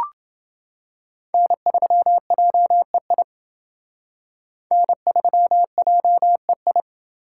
Morse code, heard as a keyed tone, about 29 wpm, 700 Hz. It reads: N3JEI N3JEI